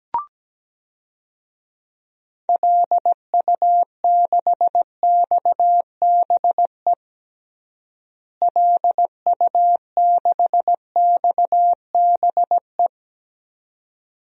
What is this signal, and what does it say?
Morse code, heard as a keyed tone, about 17 wpm, 700 Hz. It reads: LU6XBE LU6XBE
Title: DXCC-100 Call Signs - 6 Characters - Repeated 2x - 17wpm